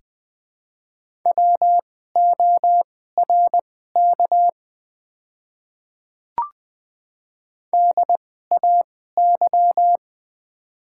Morse code, heard as a keyed tone, about 20 wpm, spaced as for 14 wpm, 700 Hz. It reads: WORK DAY
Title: 2x Character Spacing - Top 200 Words - 20wpm